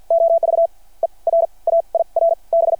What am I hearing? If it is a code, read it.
Morse code, heard as a keyed tone, about 53 wpm, spaced as for 32 wpm, 690 Hz. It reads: OVER AIR B
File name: archived / CRSnet